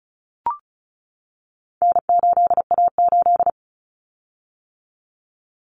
Morse code, heard as a keyed tone, about 34 wpm, 700 Hz. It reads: N8A8
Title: Common Cut Numbers - 4-Digit 35wpm